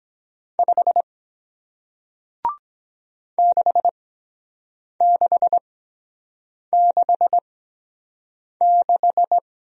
Morse code, heard as a keyed tone, about 23 wpm, spaced as for 7 wpm, 700 Hz. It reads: 5 6666